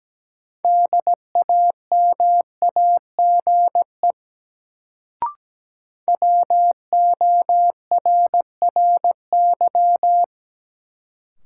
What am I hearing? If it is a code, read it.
Morse code, heard as a keyed tone, about 17 wpm, 700 Hz. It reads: DAMAGE WORRY
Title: Sentences from Top 1100 Words - New Word Drill Alt - 17wpm